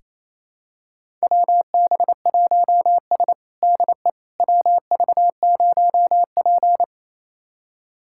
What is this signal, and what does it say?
Morse code, heard as a keyed tone, about 28 wpm, 700 Hz. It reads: WB1S DE WV0P
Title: USA Callsign DE Callsign - 4 Characters - 28wpm